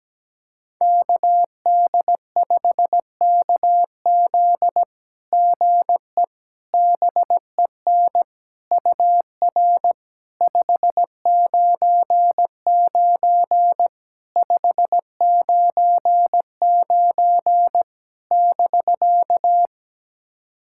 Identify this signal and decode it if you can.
Morse code, heard as a keyed tone, about 17 wpm, 700 Hz. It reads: KD5KZ GE BEN UR 599 599 <BK>
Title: SOTA QSO Practice 17wpm